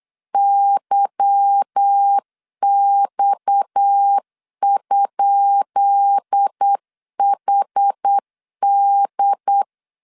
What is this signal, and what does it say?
Morse code, heard as a keyed tone, about 8 wpm, 800 Hz. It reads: YX?HD
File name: LEZIONI / 10 SIGNS / 10B INT 8wpm 3534